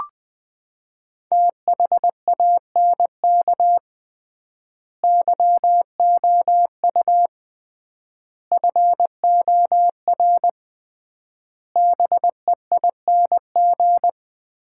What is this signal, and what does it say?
Morse code, heard as a keyed tone, about 21 wpm, 700 Hz. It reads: THANK YOU FOR BEING
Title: 3x Word-Spacing - Sets of 5 Words - 20wpm